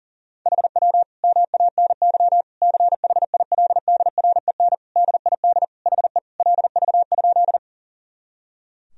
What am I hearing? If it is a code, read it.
Morse code, heard as a keyed tone, about 40 wpm, 700 Hz. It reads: HW MANY CHILDREN DID HE LV?
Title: Sentences from Top 400 Words and Abbreviations 40wpm